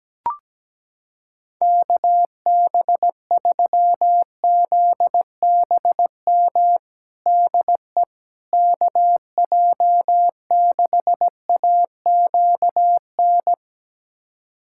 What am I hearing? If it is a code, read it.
Morse code, heard as a keyed tone, about 17 wpm, 700 Hz. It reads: KB3ZBM DE KJ6AQN
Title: USA Callsign DE Callsign - 6 Characters - 17wpm